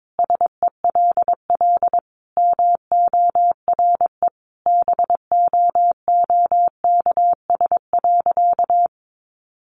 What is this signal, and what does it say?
Morse code, heard as a keyed tone, about 22 wpm, 700 Hz. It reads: SELL MORE BOOKS.
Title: Sentences from Top 1200 Words 22wpm